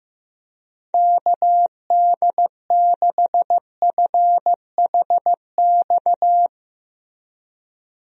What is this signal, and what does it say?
Morse code, heard as a keyed tone, about 15 wpm, 700 Hz. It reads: KD6FHX
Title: USA Call Signs - 6 Characters - Repeated 2x - 15wpm